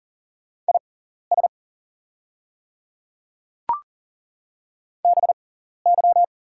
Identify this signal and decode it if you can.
Morse code, heard as a keyed tone, about 40 wpm, spaced as for 14 wpm, 700 Hz. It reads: IS BY
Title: 6x Character Spacing - Top 100 Words - 40wpm